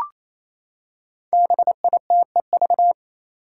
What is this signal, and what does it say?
Morse code, heard as a keyed tone, about 27 wpm, 700 Hz. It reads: BITEV
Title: Five Challenging Letter-Number Combinations 28wpm